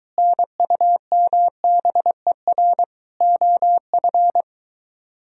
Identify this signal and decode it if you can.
Morse code, heard as a keyed tone, about 23 wpm, 700 Hz. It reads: NUMBER OF